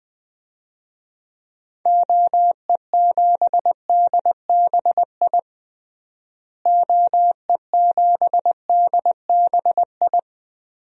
Morse code, heard as a keyed tone, about 20 wpm, 700 Hz. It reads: OE7DBI OE7DBI
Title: DXCC-100 Call Signs - 6 Characters - Repeated 3x - 20wpm